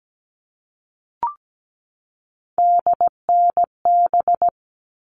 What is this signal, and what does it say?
Morse code, heard as a keyed tone, about 17 wpm, 700 Hz. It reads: DNB
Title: All Cut Numbers - 3-Digit 17wpm